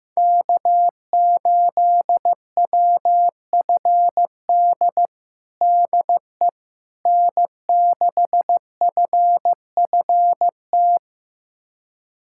Morse code, heard as a keyed tone, about 15 wpm, 700 Hz. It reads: K8WFD DE N6FFT
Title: USA Callsign DE Callsign - 5 Characters - 15wpm